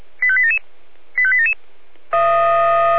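Five-tone selective call (ZVEI), heard as quick runs of five tones, tones of about 70 ms. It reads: 85877 85877
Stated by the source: ZVEI five-tone selective call 85360 received by SDR.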